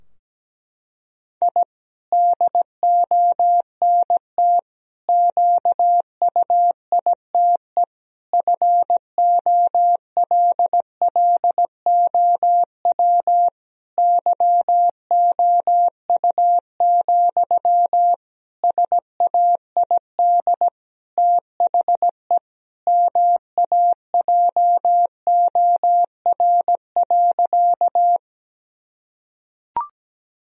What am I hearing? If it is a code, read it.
Morse code, heard as a keyed tone, about 17 wpm, 700 Hz. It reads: I DONT QUITE FOLLOW YOU, SAID THE MAJOR.
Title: Sentences from Top 1000 Words - Review Part 4 - 17wpm